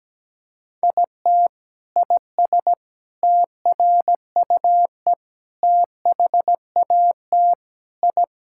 Morse code, heard as a keyed tone, about 17 wpm, 700 Hz. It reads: IT IS TRUE THAT I
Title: Sentences from Top 1000 Words - Review Part 6 - 17wpm